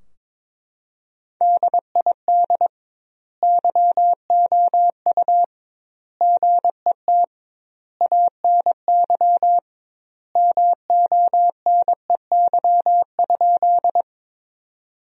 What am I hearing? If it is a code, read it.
Morse code, heard as a keyed tone, about 22 wpm, 700 Hz. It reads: DID YOU GET ANY MONEY?
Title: 2x Word-Spacing - Sentences from Top 400 Words - 22wpm